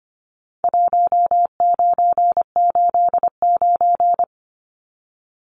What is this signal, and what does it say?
Morse code, heard as a keyed tone, about 25 wpm, 700 Hz. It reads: 1989
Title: Numbers - 4-Digit 25wpm